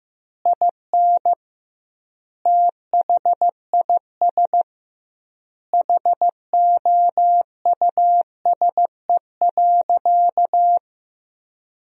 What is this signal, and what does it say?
Morse code, heard as a keyed tone, about 15 wpm, 700 Hz. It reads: IN THIS HOUSE.
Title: 2x Word-Spacing - Sentences from Top 400 Words - 15wpm